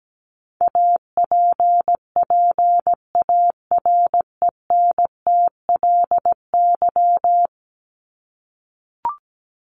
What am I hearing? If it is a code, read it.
Morse code, heard as a keyed tone, about 17 wpm, 700 Hz. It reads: APPARENTLY
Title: LY Suffix - Top 100 Words with Base Words 17wpm